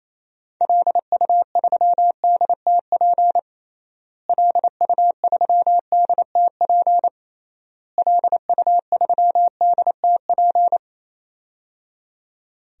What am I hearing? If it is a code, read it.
Morse code, heard as a keyed tone, about 28 wpm, 700 Hz. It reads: LU3DTP LU3DTP LU3DTP
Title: DXCC-100 Call Signs - 6 Characters - Repeated 3x - 28wpm